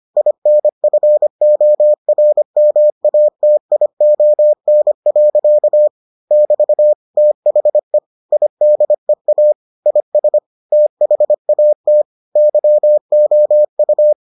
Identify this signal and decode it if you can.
Morse code, heard as a keyed tone, about 25 wpm, 600 Hz. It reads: INFORMATION. <BT> THE IDEA IS THAT YOU